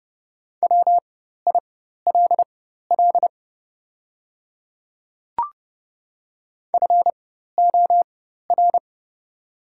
Morse code, heard as a keyed tone, about 30 wpm, spaced as for 14 wpm, 700 Hz. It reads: WILL FOR
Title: 4x Character Spacing - Top 200 Words - 30wpm